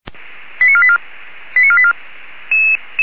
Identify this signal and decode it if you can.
Five-tone selective call (ZVEI), heard as quick runs of five tones, tones of about 70 ms. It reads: 89484 89484